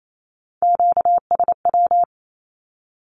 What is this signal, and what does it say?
Morse code, heard as a keyed tone, about 28 wpm, 700 Hz. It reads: QSW
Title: Q-Codes 28wpm